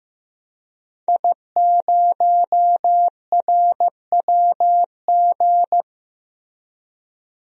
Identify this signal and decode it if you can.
Morse code, heard as a keyed tone, about 15 wpm, 700 Hz. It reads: I0RWG
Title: DXCC-100 Call Signs - 5 Characters - Repeated 2x - 15wpm